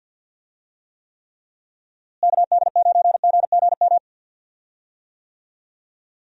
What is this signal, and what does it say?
Morse code, heard as a keyed tone, about 52 wpm, 700 Hz. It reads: KD9GGM
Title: USA Call Signs - 6 Characters - Repeated 3x - 50wpm